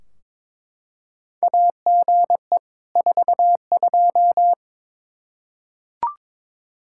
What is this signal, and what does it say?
Morse code, heard as a keyed tone, about 23 wpm, 700 Hz. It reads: AGE 42
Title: Practice QSO - Exchange 3 - More Info - 22wpm